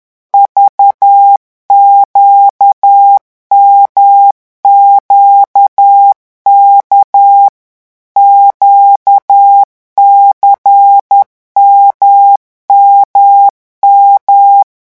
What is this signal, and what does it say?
Morse code, heard as a keyed tone, about 10 wpm, 800 Hz. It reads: VQMQK QCMMM
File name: morse mp3 / revision2-vitesse 10 mots